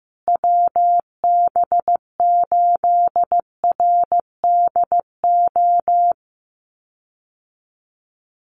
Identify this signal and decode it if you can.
Morse code, heard as a keyed tone, about 15 wpm, 700 Hz. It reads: WB8RDO
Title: USA Call Signs Repeated 2x 15wpm